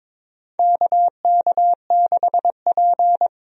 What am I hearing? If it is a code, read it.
Morse code, heard as a keyed tone, about 21 wpm, 700 Hz. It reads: KK6P
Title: USA Call Signs - 4 Characters - Repeated 2x - 22wpm